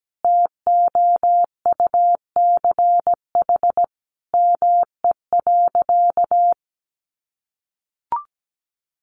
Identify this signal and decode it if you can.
Morse code, heard as a keyed tone, about 17 wpm, 700 Hz. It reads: TOUCH ME.
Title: Sentences from Top 1200 Words 17wpm